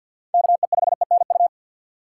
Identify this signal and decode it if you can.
Morse code, heard as a keyed tone, about 47 wpm, 700 Hz. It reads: KE5ENU
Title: USA Call Signs - 2x Character Spacing - 50wpm